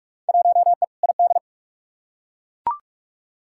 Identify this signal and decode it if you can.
Morse code, heard as a keyed tone, about 41 wpm, 700 Hz. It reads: 1E ID
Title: Field Day Exchanges - 45wpm